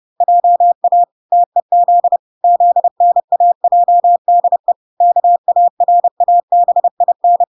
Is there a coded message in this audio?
Yes: Morse code from 0.2 to 7.4 s, heard as a keyed tone, about 30 wpm, 700 Hz. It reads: JA TEZ ZNAJDE KARABIN